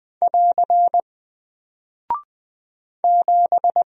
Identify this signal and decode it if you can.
Morse code, heard as a keyed tone, about 20 wpm, 700 Hz. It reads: <AR> 7
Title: Single Character (Letter:Number:Punctuation:Prosign) Alt 20wpm